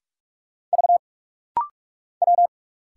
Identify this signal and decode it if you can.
Morse code, heard as a keyed tone, about 46 wpm, 700 Hz. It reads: V W
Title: Single Letter-Number Alt - Rapid-Fire 45wpm